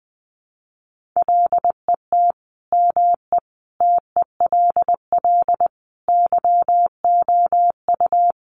Morse code, heard as a keyed tone, about 19 wpm, 700 Hz. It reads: LET ME TELL YOU
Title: Sets of 5 Words Encore 20wpm